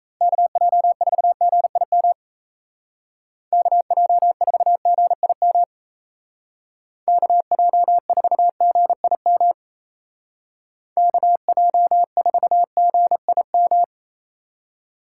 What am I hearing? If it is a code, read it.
Morse code, heard as a keyed tone, about 34 wpm, 700 Hz. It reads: KJ4GIM KJ4GIM KJ4GIM KJ4GIM
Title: Speed Racing - USA Call Signs 28wpm